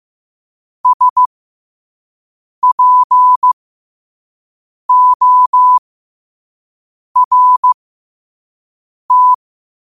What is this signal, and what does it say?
Morse code, heard as a keyed tone, about 15 wpm, spaced as for 5 wpm, 1000 Hz. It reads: SPORT
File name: qotdfiles / 5wpm